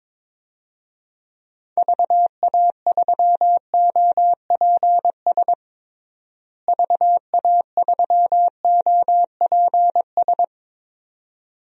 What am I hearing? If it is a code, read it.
Morse code, heard as a keyed tone, about 22 wpm, 700 Hz. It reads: VA3OPS VA3OPS
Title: DXCC-100 Call Signs - 6 Characters - Repeated 2x - 22wpm